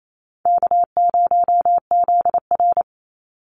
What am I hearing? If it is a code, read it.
Morse code, heard as a keyed tone, about 28 wpm, 700 Hz. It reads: K0ZR
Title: Easy Call Signs - 28wpm